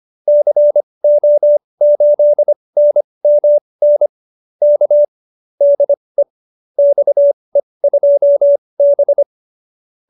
Morse code, heard as a keyed tone, about 25 wpm, spaced as for 20 wpm, 590 Hz. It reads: CO8NMN K DE XE2B